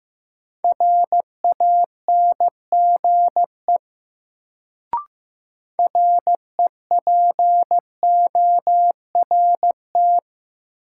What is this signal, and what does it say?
Morse code, heard as a keyed tone, about 15 wpm, 700 Hz. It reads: RANGE REPORT
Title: Sentences from Top 900 Words - New Word Drill - 15wpm